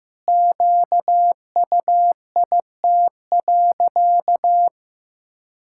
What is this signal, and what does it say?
Morse code, heard as a keyed tone, about 16 wpm, 700 Hz. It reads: QUIT.
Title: Pangram Challenge v2 - 15wpm